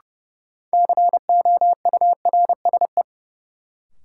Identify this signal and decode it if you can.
Morse code, heard as a keyed tone, about 30 wpm, 700 Hz. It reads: COURSE
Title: Sentences from Top 500 Words - New Word Drill - 30wpm